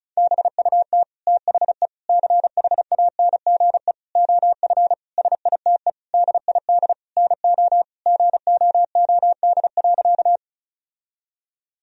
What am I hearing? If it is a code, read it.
Morse code, heard as a keyed tone, about 34 wpm, 700 Hz. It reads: BUT THE CHANGE OF SITE DID NO GOOD.
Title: Sentences from Top 1000 Words - Review Part 4 - 35wpm